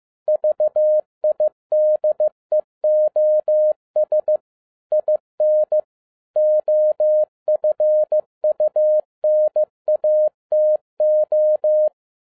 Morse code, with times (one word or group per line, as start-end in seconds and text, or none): 0.3-4.4 VIDEOS
4.9-5.8 IN
6.4-11.9 OFUNATO